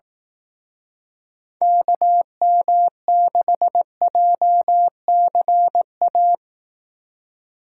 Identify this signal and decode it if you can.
Morse code, heard as a keyed tone, about 18 wpm, 700 Hz. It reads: KM6JCA
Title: Speed Racing - USA Call Signs 15wpm